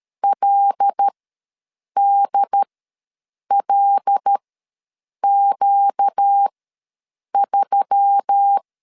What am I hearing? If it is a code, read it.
Morse code, heard as a keyed tone, about 13 wpm, spaced as for 7 wpm, 800 Hz. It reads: LDLQ3
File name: LEZIONI / 5 LBVC / 5 L ws 12wpm 2958